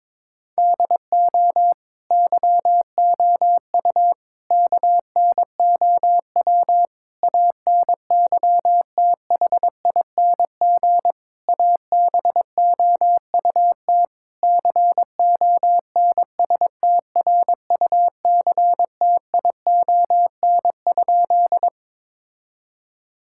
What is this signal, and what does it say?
Morse code, heard as a keyed tone, about 22 wpm, 700 Hz. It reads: DO YOU KNOW ANYTHING ABOUT CONSTRUCTION?
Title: Sentences from Top 1600 Words 22wpm